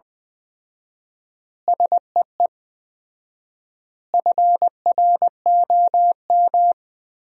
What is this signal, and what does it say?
Morse code, heard as a keyed tone, about 20 wpm, 700 Hz. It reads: SEE FROM
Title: 4x Word-Spacing - Sentences from Top 100 Words - 20wpm